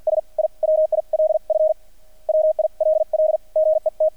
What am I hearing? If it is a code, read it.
Morse code, heard as a keyed tone, about 50 wpm, spaced as for 43 wpm, 630 Hz. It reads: S I WIL F WILL GET
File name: archived / CRSnet